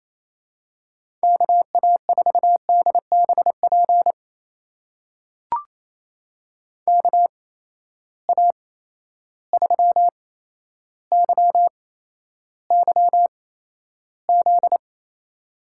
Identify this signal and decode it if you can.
Morse code, heard as a keyed tone, about 28 wpm, spaced as for 12 wpm, 700 Hz. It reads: KA4DBP KA3YYZ
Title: USA Call Signs - 6 Characters - 8x Character Spacing - 28wpm